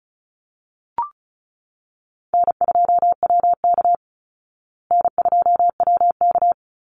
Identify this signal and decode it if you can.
Morse code, heard as a keyed tone, about 34 wpm, 700 Hz. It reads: N2WK N2WK
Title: Easy Call Signs Repeated 2x - 35wpm